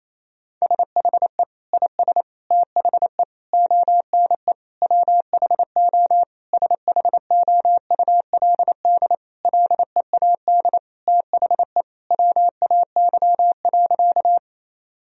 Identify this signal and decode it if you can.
Morse code, heard as a keyed tone, about 27 wpm, 700 Hz. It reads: SHE IS THE ONE WHO SHOULD LEAD THE WAY.
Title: Sentences from Top 1000 Words - Review Part 2 - 28wpm